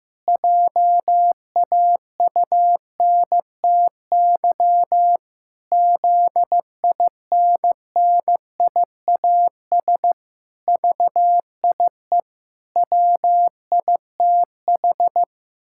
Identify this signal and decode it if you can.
Morse code, heard as a keyed tone, about 15 wpm, 700 Hz. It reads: JAUNTY ZINNIAS VIE WITH
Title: Pangram Challenge v2 - 15wpm